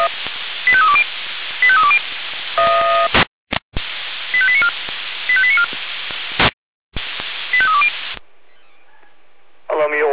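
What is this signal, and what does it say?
Five-tone selective call (ZVEI), heard as quick runs of five tones, tones of about 70 ms. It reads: 85320 85320 85894 85894 85320
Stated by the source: Five-tone ZVEI selective call received on SDR.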